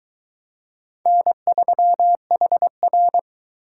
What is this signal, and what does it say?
Morse code, heard as a keyed tone, about 20 wpm, 700 Hz. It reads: N3HR
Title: Speed Racing - USA 1x2 Call Signs 17wpm